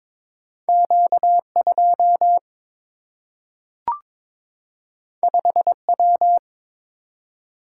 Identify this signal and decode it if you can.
Morse code, heard as a keyed tone, about 22 wpm, 700 Hz. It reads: Q 2 5 W
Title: Single-Letter Single-Number 22wpm